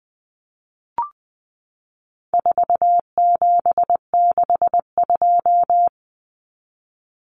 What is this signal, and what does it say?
Morse code, heard as a keyed tone, about 20 wpm, 700 Hz. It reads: 4762
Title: Numbers - 4-Digit 20wpm